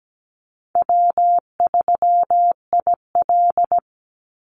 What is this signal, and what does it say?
Morse code, heard as a keyed tone, about 17 wpm, 700 Hz. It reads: W3IL
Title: Easy Call Signs - 17wpm